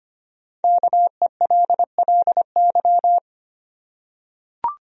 Morse code, heard as a keyed tone, about 25 wpm, 700 Hz. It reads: KELLY
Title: USA Common Names - 25wpm